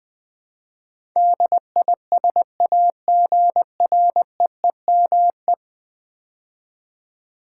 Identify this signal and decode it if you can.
Morse code, heard as a keyed tone, about 20 wpm, 700 Hz. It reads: DISAGREEME
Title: Build A Word - 5 Letters At A Time - 12-Letter Words - 20wpm